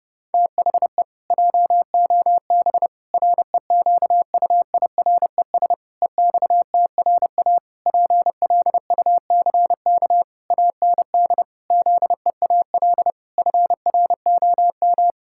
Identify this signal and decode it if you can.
Morse code, heard as a keyed tone, about 30 wpm, 700 Hz. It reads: THE JOB REQUIRES EXTRA PLUCK AND ZEAL FROM
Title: Pangram Challenge v2 - 30wpm